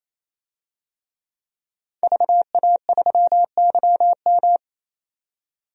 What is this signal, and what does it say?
Morse code, heard as a keyed tone, about 28 wpm, 700 Hz. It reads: VA3YM
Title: DXCC-100 Call Signs - 5 Characters - 28wpm